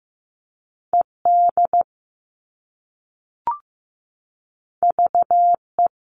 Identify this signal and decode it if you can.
Morse code, heard as a keyed tone, about 15 wpm, 700 Hz. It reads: ED VE
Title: All Cut Numbers - 2-Digit 15wpm